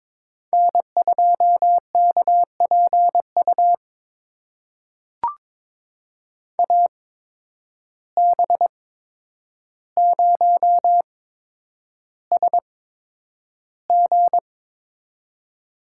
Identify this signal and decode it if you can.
Morse code, heard as a keyed tone, about 22 wpm, spaced as for 9 wpm, 700 Hz. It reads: N2KPU AB0SG
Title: USA Call Signs - 5 Characters - 8x Character Spacing - 22wpm